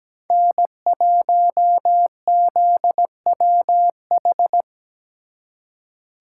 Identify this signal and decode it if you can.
Morse code, heard as a keyed tone, about 20 wpm, 700 Hz. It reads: N1ZWH
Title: Speed Racing - USA Call Signs 17wpm